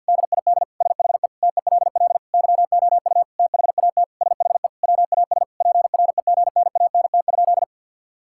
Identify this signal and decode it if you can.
Morse code, heard as a keyed tone, about 50 wpm, 700 Hz. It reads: DID SHE TELL YOU THAT SHE WAS PREGNANT?
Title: Sentences from Top 1000 Words - Review Part 6 - 50wpm